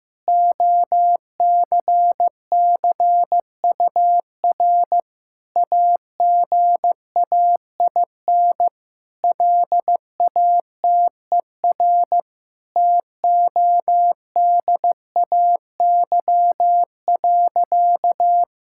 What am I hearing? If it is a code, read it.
Morse code, heard as a keyed tone, about 15 wpm, 700 Hz. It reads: OCCUR AGAIN LATER TODAY.
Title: Sentences from Top 1000 Words - Review Part 6 - 15wpm